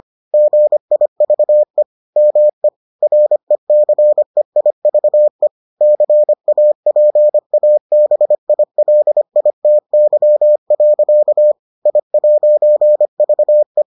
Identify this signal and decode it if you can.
Morse code, heard as a keyed tone, about 25 wpm, 600 Hz. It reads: GIVE ME RECEIVE CAPABILITY. I'VE